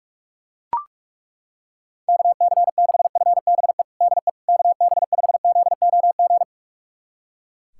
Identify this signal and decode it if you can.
Morse code, heard as a keyed tone, about 41 wpm, 700 Hz. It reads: KC6FBE DE KB5ZOG
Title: USA Callsign DE Callsign - 6 Characters - 45wpm